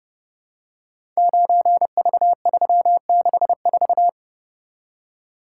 Morse code, heard as a keyed tone, about 30 wpm, 700 Hz. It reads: 9V364
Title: Code Groups - 5 Letter:Numbers - 30wpm